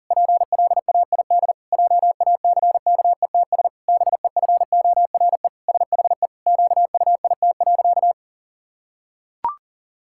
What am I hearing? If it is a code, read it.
Morse code, heard as a keyed tone, about 40 wpm, 700 Hz. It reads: PLAID JACKETS BEFORE SHE QUIT.
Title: Pangram Challenge v2 - 40wpm